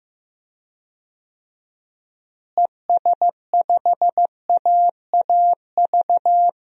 Morse code, heard as a keyed tone, about 15 wpm, 700 Hz. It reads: ES5AAV
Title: DXCC-100 Call Signs - 6 Characters - Repeated 3x - 15wpm